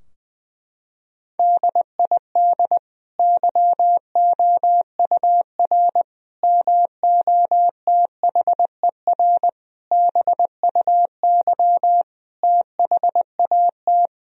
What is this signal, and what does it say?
Morse code, heard as a keyed tone, about 20 wpm, 700 Hz. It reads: DID YOUR MOTHER BUY THAT
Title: Sentences from Top 1000 Words - Review Part 3 - 20wpm